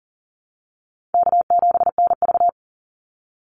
Morse code, heard as a keyed tone, about 40 wpm, 700 Hz. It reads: K7NV
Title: Easy Call Signs Repeated 2x - 40wpm